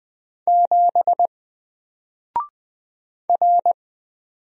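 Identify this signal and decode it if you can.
Morse code, heard as a keyed tone, about 20 wpm, 700 Hz. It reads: 7 R
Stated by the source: Single letter and number Morse drill at 20 wpm, from N4LCD.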